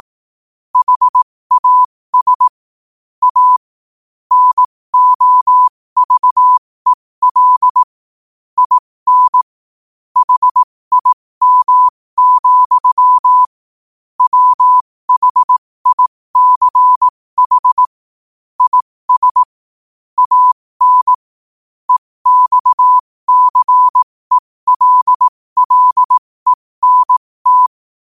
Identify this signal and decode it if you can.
Morse code, heard as a keyed tone, about 18 wpm, spaced as for 15 wpm, 1000 Hz. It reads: HAS A NOVEL IN HIM, WHICH IS AN EXCELLENT
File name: qotdfiles / 15wpm